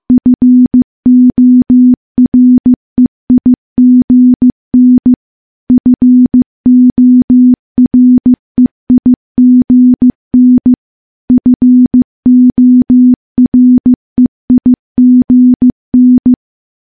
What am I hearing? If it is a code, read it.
Morse code, heard as a keyed tone, about 15 wpm, 250 Hz. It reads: FOREIGN FOREIGN FOREIGN
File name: morse-training-15WPM